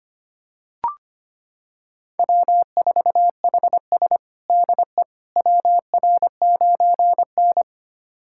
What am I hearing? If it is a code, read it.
Morse code, heard as a keyed tone, about 25 wpm, 700 Hz. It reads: W4HS DE WR9N
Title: USA Callsign DE Callsign - 4 Characters - 25wpm